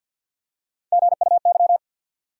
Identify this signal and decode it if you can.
Morse code, heard as a keyed tone, about 53 wpm, 700 Hz. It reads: GUY
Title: Three Letter Words - 50wpm